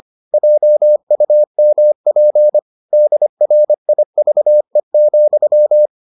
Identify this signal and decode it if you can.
Morse code, heard as a keyed tone, about 25 wpm, 600 Hz. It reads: JUMP DRIVE,